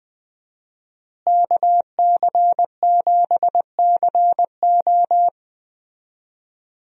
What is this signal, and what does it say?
Morse code, heard as a keyed tone, about 20 wpm, 700 Hz. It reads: KC7CO
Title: USA Call Signs - 5 Characters - Repeated 2x - 20wpm